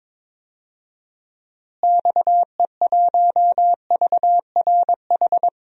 Morse code, heard as a keyed tone, about 22 wpm, 700 Hz. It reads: XE1VRH
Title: DXCC-100 Call Signs - 6 Characters - Repeated 3x - 22wpm